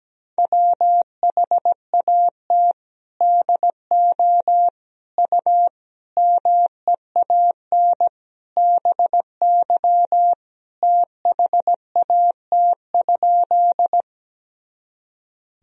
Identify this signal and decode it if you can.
Morse code, heard as a keyed tone, about 17 wpm, 700 Hz. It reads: WHAT DO U MEAN BY THAT?
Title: Sentences from Top 200 Words and Abbreviations 17wpm